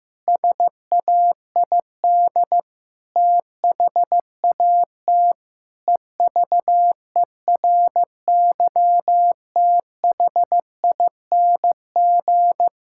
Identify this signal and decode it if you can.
Morse code, heard as a keyed tone, about 16 wpm, 700 Hz. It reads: SAID THAT EVERYTHING
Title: Pangram Challenge v2 - 15wpm